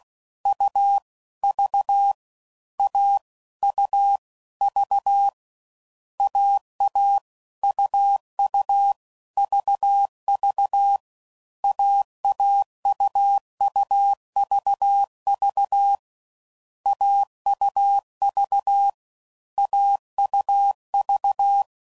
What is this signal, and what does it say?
Morse code, heard as a keyed tone, about 16 wpm, 800 Hz. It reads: U V A U V AA UU VV AAUUVV AUV AUV
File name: Combinations / Difficult Combos / AUV AUV4